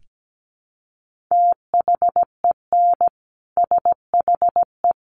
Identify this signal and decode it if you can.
Morse code, heard as a keyed tone, about 17 wpm, 700 Hz. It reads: THEN SHE
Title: Sentences from Top 1300 Words 17wpm